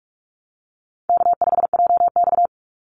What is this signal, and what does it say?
Morse code, heard as a keyed tone, about 44 wpm, 700 Hz. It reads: K5JX
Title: Call Signs - 45wpm